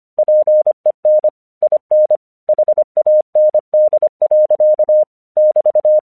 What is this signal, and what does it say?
Morse code, heard as a keyed tone, about 24 wpm, 610 Hz. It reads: PEN IN HAND. <BT>
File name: English 25wpm